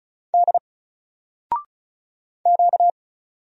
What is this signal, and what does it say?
Morse code, heard as a keyed tone, about 36 wpm, 700 Hz. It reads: D Q
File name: Single-Letter-Number-35wpm